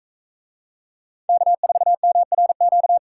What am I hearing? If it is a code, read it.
Morse code, heard as a keyed tone, about 43 wpm, 700 Hz. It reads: K4MRQ